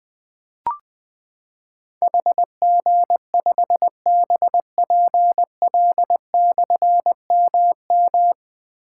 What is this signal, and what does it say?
Morse code, heard as a keyed tone, about 20 wpm, 700 Hz. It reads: HG5BPL/MM
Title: Difficult Call Signs 20wpm